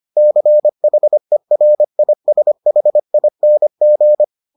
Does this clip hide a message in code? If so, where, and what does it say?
Yes: Morse code from 0.2 to 4.2 s, heard as a keyed tone, about 25 wpm, 600 Hz. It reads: CHERISHING